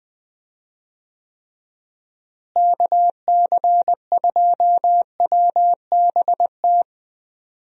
Morse code, heard as a keyed tone, about 20 wpm, 700 Hz. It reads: KC2WBT
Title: USA Call Signs - 6 Characters - Repeated 3x - 20wpm